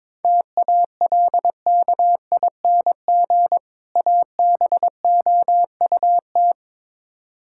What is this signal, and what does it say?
Morse code, heard as a keyed tone, about 22 wpm, 700 Hz. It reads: TALKING ABOUT
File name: Build A Phrase - 2-Word Phrases - 22wpm